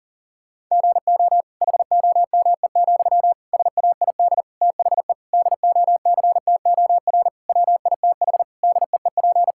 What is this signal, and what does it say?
Morse code, heard as a keyed tone, about 37 wpm, 700 Hz. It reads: GO HOME, SAID THE DOCTOR WITH DEEP
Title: Sentences from Top 1600 Words 40wpm